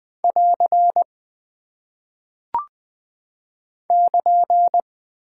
Morse code, heard as a keyed tone, about 20 wpm, 700 Hz. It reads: <AR> <KN>
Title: Single Punctuation-Prosign - 20wpm